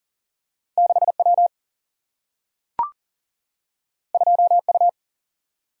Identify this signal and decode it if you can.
Morse code, heard as a keyed tone, about 39 wpm, 700 Hz. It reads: 6 W 2 U
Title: Single-Letter Single-Number 40wpm